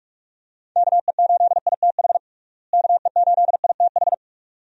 Morse code, heard as a keyed tone, about 41 wpm, 700 Hz. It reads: KE8ITH KE8ITH
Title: USA Call Signs - 6 Characters - Repeated 2x - 45wpm